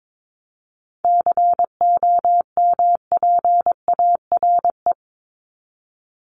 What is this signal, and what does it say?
Morse code, heard as a keyed tone, about 22 wpm, 700 Hz. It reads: COMPARE
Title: Top 1251-1500 Words 22wpm